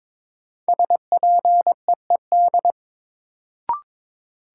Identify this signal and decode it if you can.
Morse code, heard as a keyed tone, about 22 wpm, 700 Hz. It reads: SPEED
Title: Sentences from Top 800 Words - New Word Drill - 22wpm